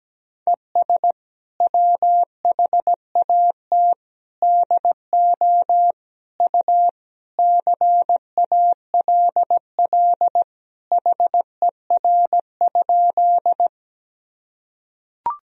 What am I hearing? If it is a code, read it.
Morse code, heard as a keyed tone, about 17 wpm, 700 Hz. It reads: ES WHAT DO U CALL HER?
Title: Sentences from Top 100 Words and Abbreviations 17wpm